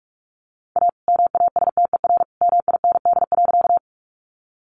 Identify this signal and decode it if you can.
Morse code, heard as a keyed tone, about 44 wpm, 700 Hz. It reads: A MASTER MIND.
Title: Sentences from Top 1300 Words 45wpm